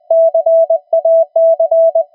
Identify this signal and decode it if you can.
Morse code, heard as a keyed tone, about 20 wpm, 660 Hz. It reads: CAC
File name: attachments / TX 28060